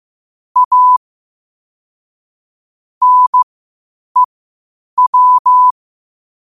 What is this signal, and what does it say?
Morse code, heard as a keyed tone, about 15 wpm, spaced as for 8 wpm, 1000 Hz. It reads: A NEW
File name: qotdfiles / 8wpm